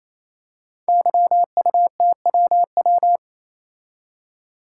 Morse code, heard as a keyed tone, about 29 wpm, 700 Hz. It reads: YUTWW
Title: Code Groups - 5 Letters - 28wpm